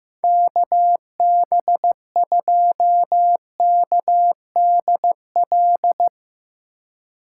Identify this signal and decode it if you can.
Morse code, heard as a keyed tone, about 15 wpm, 700 Hz. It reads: KB2KDL
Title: USA Call Signs - 6 Characters - 3x Character Spacing - 15wpm